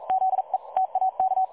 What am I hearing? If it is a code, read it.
Morse code, heard as a keyed tone, about 42 wpm, 760 Hz. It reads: ZE TAK
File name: dlazaawansowanych / nagraniazpasma / websdr recording 3565.0kHz